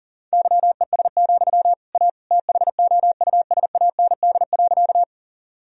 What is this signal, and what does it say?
Morse code, heard as a keyed tone, about 40 wpm, 700 Hz. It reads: YES, A THOUSAND.
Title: Sentences from Top 1000 Words - Review Part 3 - 40wpm